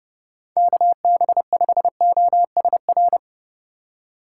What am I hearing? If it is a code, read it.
Morse code, heard as a keyed tone, about 29 wpm, 700 Hz. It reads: KB5OSR